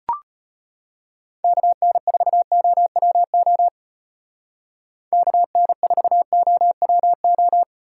Morse code, heard as a keyed tone, about 30 wpm, 700 Hz. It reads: KN4OWO KN4OWO
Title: Speed Racing - USA 2x3 Call Signs 25wpm